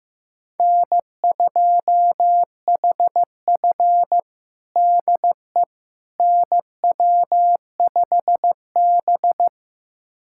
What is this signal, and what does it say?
Morse code, heard as a keyed tone, about 15 wpm, 700 Hz. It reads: N2HF DE NW5B
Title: USA Callsign DE Callsign - 4 Characters - 15wpm